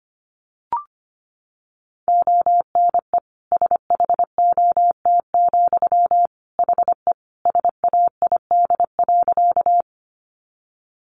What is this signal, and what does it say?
Morse code, heard as a keyed tone, about 25 wpm, 700 Hz. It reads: ONE SHOT, HE SAID.
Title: Sentences from Top 1100 Words 25wpm